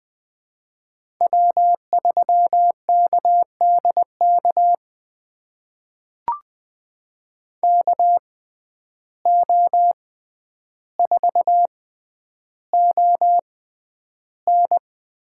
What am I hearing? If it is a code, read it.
Morse code, heard as a keyed tone, about 20 wpm, spaced as for 10 wpm, 700 Hz. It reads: W3KDK KO4ON
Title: USA Call Signs - 5 Characters - 6x Character Spacing - 20wpm